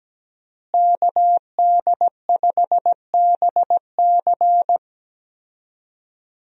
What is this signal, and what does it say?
Morse code, heard as a keyed tone, about 17 wpm, 700 Hz. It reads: KD5BC
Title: USA Call Signs - 5 Characters - Repeated 2x - 17wpm